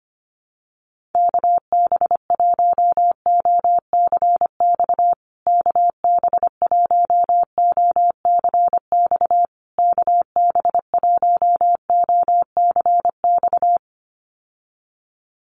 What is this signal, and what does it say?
Morse code, heard as a keyed tone, about 25 wpm, 700 Hz. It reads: KB1OCX KB1OCX KB1OCX
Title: USA Call Signs Repeated 3x 25wpm